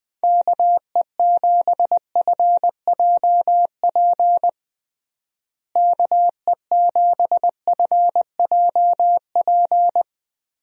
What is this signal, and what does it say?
Morse code, heard as a keyed tone, about 20 wpm, 700 Hz. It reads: KE7FJP KE7FJP
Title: USA Call Signs - 6 Characters - Repeated 3x - 20wpm